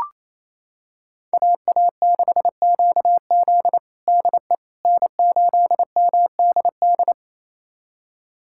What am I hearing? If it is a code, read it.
Morse code, heard as a keyed tone, about 27 wpm, 700 Hz. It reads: AA6QZ DE N8MDD
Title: USA Callsign DE Callsign - 5 Characters - 28wpm